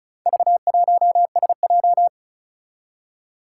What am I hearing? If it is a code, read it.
Morse code, heard as a keyed tone, about 36 wpm, 700 Hz. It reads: V1SJ
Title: Four Challenging Letter-Number Combinations 35wpm